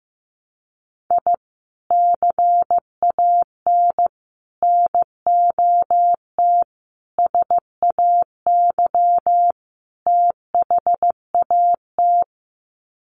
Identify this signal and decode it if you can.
Morse code, heard as a keyed tone, about 15 wpm, 700 Hz. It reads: I CAN NOT SAY THAT
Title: Sets of 5 Words Encore2 15wpm